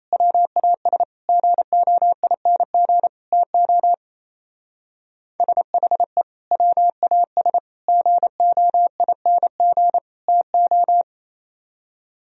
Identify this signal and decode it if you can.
Morse code, heard as a keyed tone, about 32 wpm, 700 Hz. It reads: WAS GOING TO SHE WAS GOING TO
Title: Speed-Racing - Sets of 4 Words 28wpm